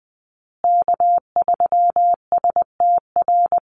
Morse code, heard as a keyed tone, about 20 wpm, 700 Hz. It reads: K3STR
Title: USA Call Signs Repeated 2x 20wpm